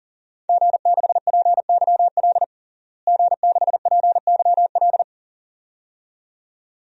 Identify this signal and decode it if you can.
Morse code, heard as a keyed tone, about 40 wpm, 700 Hz. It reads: G6PYL G6PYL
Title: DXCC-100 Call Signs - 5 Characters - Repeated 2x - 40wpm